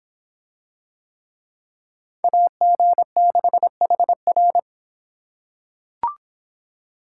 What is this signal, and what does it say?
Morse code, heard as a keyed tone, about 25 wpm, 700 Hz. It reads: AG6HR